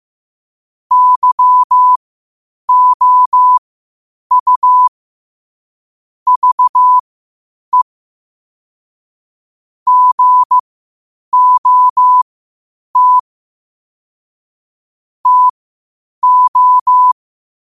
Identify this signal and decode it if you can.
Morse code, heard as a keyed tone, about 15 wpm, spaced as for 8 wpm, 1000 Hz. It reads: YOU VE GOT TO
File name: qotdfiles / 8wpm